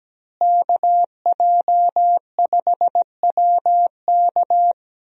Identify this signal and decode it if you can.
Morse code, heard as a keyed tone, about 17 wpm, 700 Hz. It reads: KJ5WK
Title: USA Call Signs - 5 Characters - Repeated 2x - 17wpm